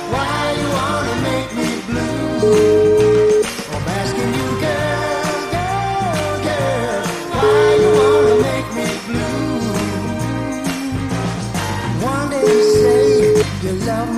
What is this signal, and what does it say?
Ringback tone (Europe, 425 Hz), with noise.